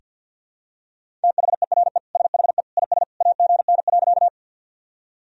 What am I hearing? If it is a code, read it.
Morse code, heard as a keyed tone, about 47 wpm, 700 Hz. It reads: THERE SHE IS AGN.
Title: Sentences from Top 200 Words and Abbreviations 50wpm